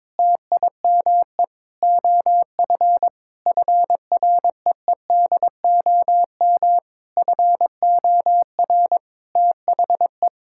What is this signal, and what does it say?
Morse code, heard as a keyed tone, about 22 wpm, 700 Hz. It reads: TIME OF FREEDOM FOR THE
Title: Sentences from Top 1000 Words - Review Part 5 - 22wpm